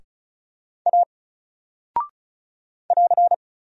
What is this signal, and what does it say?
Morse code, heard as a keyed tone, about 36 wpm, 700 Hz. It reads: A <AR>
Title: Single Character (Letter:Number:Punctuation:Prosign) 35wpm